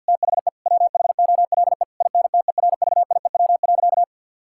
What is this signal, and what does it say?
Morse code, heard as a keyed tone, about 50 wpm, 700 Hz. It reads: THE WHOLE INTERVIEW.
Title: Sentences from Top 1600 Words 50wpm